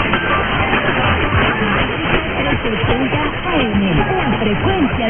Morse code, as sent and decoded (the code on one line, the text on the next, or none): none